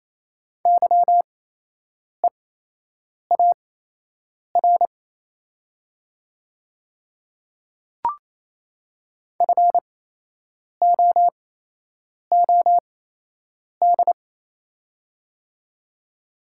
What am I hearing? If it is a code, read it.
Morse code, heard as a keyed tone, about 28 wpm, spaced as for 8 wpm, 700 Hz. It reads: YEAR FOOD
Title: 8x Character Spacing - Top 300 Words - 28wpm